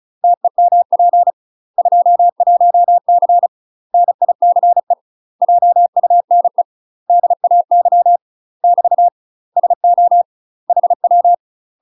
Morse code, heard as a keyed tone, about 36 wpm, 700 Hz. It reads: TEMP 21C NICE JUNE DAY <BT> SO HW
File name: UK MASTER QSOs TRAINING / 35WPM / UK MASTER QSOs 0023